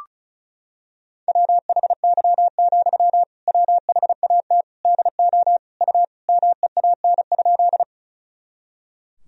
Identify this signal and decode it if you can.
Morse code, heard as a keyed tone, about 36 wpm, 700 Hz. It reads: WHY, WHAT DO U MEAN?
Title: Sentences from Top 200 Words and Abbreviations 35wpm